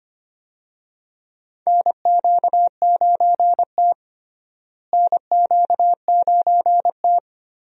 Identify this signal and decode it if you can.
Morse code, heard as a keyed tone, about 25 wpm, 700 Hz. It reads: NQ9T NQ9T
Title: USA Call Signs - 4 Characters - Repeated 2x - 25wpm